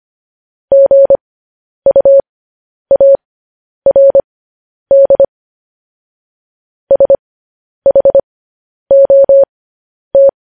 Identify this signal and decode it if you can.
Morse code, heard as a keyed tone, about 25 wpm, spaced as for 10 wpm, 550 Hz. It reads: GUARD SHOT